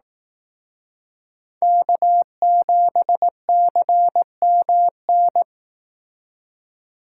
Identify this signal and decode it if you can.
Morse code, heard as a keyed tone, about 18 wpm, 700 Hz. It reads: K7CMN
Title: Speed Racing - USA 1x3 Call Signs 15wpm